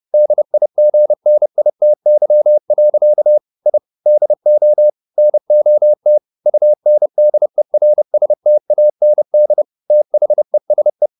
Morse code, heard as a keyed tone, about 30 wpm, 600 Hz. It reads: DIGNITY. I DO NOT UNDERSTAND THESE